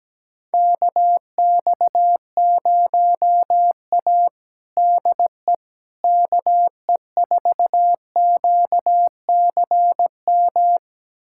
Morse code, heard as a keyed tone, about 17 wpm, 700 Hz. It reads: KX0A DE KE4QCM
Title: USA Callsign DE Callsign 17wpm